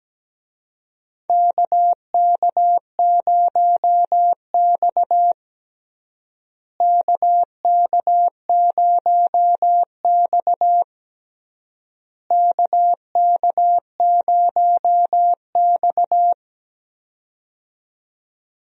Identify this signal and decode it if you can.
Morse code, heard as a keyed tone, about 17 wpm, 700 Hz. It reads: KK0X KK0X KK0X
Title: USA Call Signs - 4 Characters - Repeated 3x - 17wpm